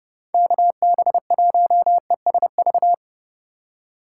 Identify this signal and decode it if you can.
Morse code, heard as a keyed tone, about 30 wpm, 700 Hz. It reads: KB1ESV